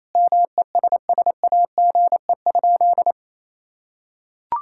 Morse code, heard as a keyed tone, about 28 wpm, 700 Hz. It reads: MESSAGE?
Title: Sentences from Top 1600 Words 28wpm